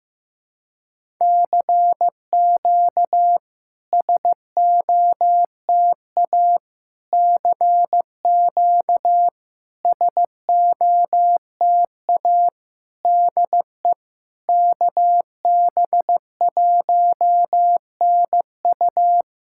Morse code, heard as a keyed tone, about 15 wpm, 700 Hz. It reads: CQ SOTA CQ SOTA DE KB1NU
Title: SOTA QSO Practice 15wpm